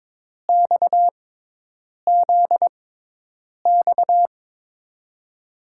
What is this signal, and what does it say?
Morse code, heard as a keyed tone, about 22 wpm, spaced as for 8 wpm, 700 Hz. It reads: XZX
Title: USA Call Signs - 5 Characters - 6x Character Spacing - 22wpm